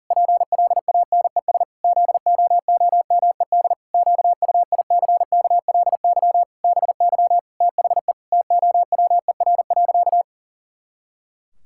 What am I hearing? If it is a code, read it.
Morse code, heard as a keyed tone, about 37 wpm, 700 Hz. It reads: PLANES ZOOMED QUICKLY BY THE TOWER.
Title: Pangram Challenge v2 - 40wpm